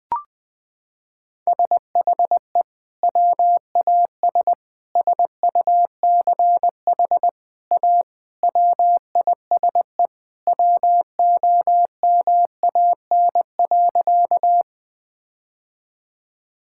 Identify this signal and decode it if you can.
Morse code, heard as a keyed tone, about 20 wpm, 700 Hz. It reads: SHE WAS SUCH A WISE WOMAN.
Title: Sentences from Top 1600 Words 20wpm